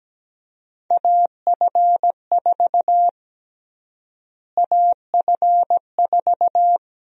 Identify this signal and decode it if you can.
Morse code, heard as a keyed tone, about 17 wpm, 700 Hz. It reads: AF4 AF4
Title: USA Call Sign Prefixes - Repeated 2x - 17wpm